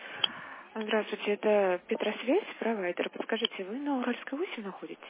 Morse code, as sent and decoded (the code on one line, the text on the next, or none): none